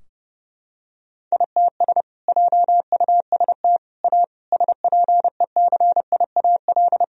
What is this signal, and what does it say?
Morse code, heard as a keyed tone, about 30 wpm, 700 Hz. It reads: ITS JUST A SPECIAL
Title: Sentences from Top 1000 Words - Review Part 4 - 30wpm